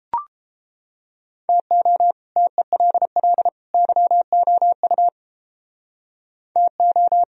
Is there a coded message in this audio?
Yes: Morse code from 1.5 to 7.2 s, heard as a keyed tone, about 29 wpm, 700 Hz. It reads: TO TELL YOU TO